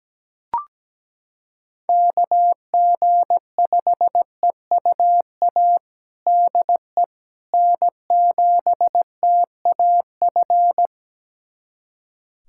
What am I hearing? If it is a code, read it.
Morse code, heard as a keyed tone, about 17 wpm, 700 Hz. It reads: KG5EUA DE N7TAF
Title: USA Callsign DE Callsign 17wpm